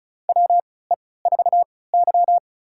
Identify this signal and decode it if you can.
Morse code, heard as a keyed tone, about 36 wpm, spaced as for 20 wpm, 700 Hz. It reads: WE4Y
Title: USA Call Signs - 4 Characters - 3x Character Spacing - 35wpm